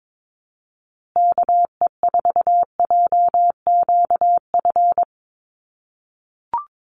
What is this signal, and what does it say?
Morse code, heard as a keyed tone, about 22 wpm, 700 Hz. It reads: KE4JQF
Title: USA Call Signs Repeated 3x 22wpm